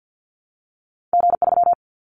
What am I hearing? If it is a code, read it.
Morse code, heard as a keyed tone, about 50 wpm, 700 Hz. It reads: G3
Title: Callsign Prefixes for Top 25 DXCC - Repeated 3x - 50wpm